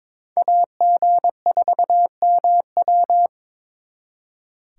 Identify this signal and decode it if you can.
Morse code, heard as a keyed tone, about 26 wpm, 700 Hz. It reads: AG4MW
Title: Speed Racing - USA 2x2 Call Signs 22wpm